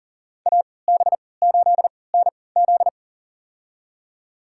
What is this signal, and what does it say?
Morse code, heard as a keyed tone, about 40 wpm, spaced as for 23 wpm, 700 Hz. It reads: AB8NZ
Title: USA Call Signs - 5 Characters - 3x Character Spacing - 40wpm